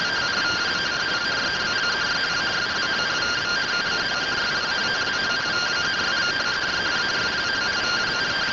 Two-tone teleprinter signal, about 170 Hz shift.